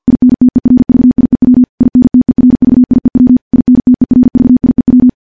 Morse code, heard as a keyed tone, about 46 wpm, 250 Hz. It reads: INTERVIEW INTERVIEW INTERVIEW
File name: morse / morse-training-50WPM